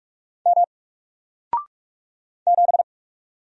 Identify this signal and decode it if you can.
Morse code, heard as a keyed tone, about 46 wpm, 700 Hz. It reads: M 7
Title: Single Letter-Number Alt 45wpm